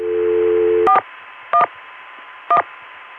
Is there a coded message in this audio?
Yes: DTMF from 0.9 to 2.6 s, heard as telephone touch tones, tones of about 80 ms. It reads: 411